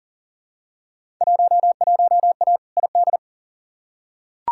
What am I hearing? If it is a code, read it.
Morse code, heard as a keyed tone, about 40 wpm, 700 Hz. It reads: 11A ID
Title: Field Day Exchanges Encore - 40wpm